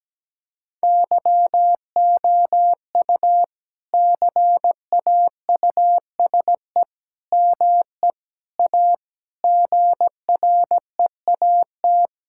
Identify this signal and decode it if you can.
Morse code, heard as a keyed tone, about 17 wpm, 700 Hz. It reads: YOU CAUSE ME A GREAT
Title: Sentences from Top 1600 Words 17wpm